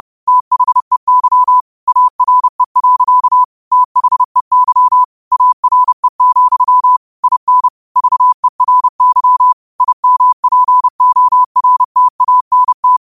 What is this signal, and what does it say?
Morse code, heard as a keyed tone, about 30 wpm, 1000 Hz. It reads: THEY ARE. THEY ARE, IN VERY IMPORTANT
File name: qotdfiles / 30wpm